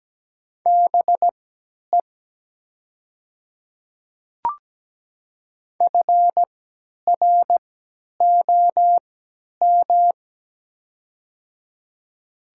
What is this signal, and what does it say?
Morse code, heard as a keyed tone, about 17 wpm, spaced as for 10 wpm, 700 Hz. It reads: BE FROM
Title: 3x Character Spacing - Top 300 Words - 17wpm